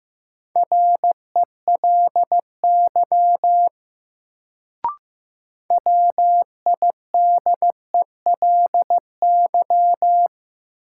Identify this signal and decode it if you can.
Morse code, heard as a keyed tone, about 15 wpm, 700 Hz. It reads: RELY WIDELY
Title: Sentences from Top 1200 Words - New Word Drill Alt - 15wpm